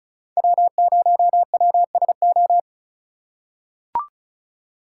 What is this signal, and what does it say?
Morse code, heard as a keyed tone, about 34 wpm, 700 Hz. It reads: W0WSO
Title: USA Call Signs - 5 Characters - 4x Character Spacing - 35wpm